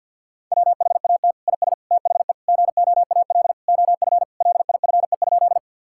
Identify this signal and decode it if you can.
Morse code, heard as a keyed tone, about 50 wpm, 700 Hz. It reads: WHAT IS THE GOAL OF LIFE?
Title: Sentences from Top 1000 Words - Review Part 5 - 50wpm